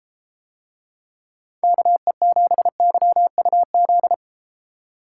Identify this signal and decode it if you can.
Morse code, heard as a keyed tone, about 32 wpm, 700 Hz. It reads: KE7YUZ